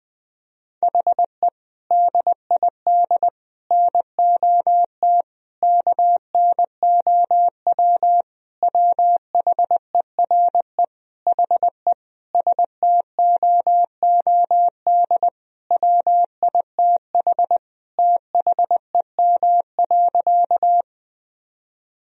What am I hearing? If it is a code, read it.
Morse code, heard as a keyed tone, about 20 wpm, 700 Hz. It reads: HE DID NOT KNOW WHERE HE STOOD WITH THEM.
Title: Sentences from Top 1000 Words - Review Part 2 - 20wpm